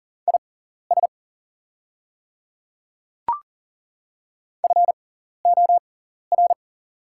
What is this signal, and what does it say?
Morse code, heard as a keyed tone, about 40 wpm, spaced as for 14 wpm, 700 Hz. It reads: IS FOR
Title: 6x Character Spacing - Top 200 Words - 40wpm